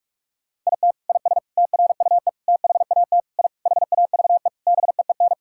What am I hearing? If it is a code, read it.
Morse code, heard as a keyed tone, about 46 wpm, 700 Hz. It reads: IT IS TRUE THAT I HAVE BEEN
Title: Sentences from Top 1000 Words - Review Part 6 - 45wpm